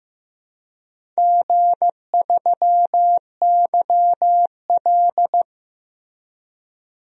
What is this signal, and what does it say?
Morse code, heard as a keyed tone, about 15 wpm, 700 Hz. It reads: G3YL
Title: DXCC-100 Call Signs - 4 Characters - 15wpm